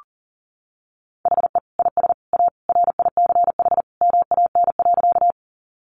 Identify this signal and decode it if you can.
Morse code, heard as a keyed tone, about 37 wpm, 700 Hz. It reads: HE IS A RICH MAN.
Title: Sentences from Top 1100 Words 40wpm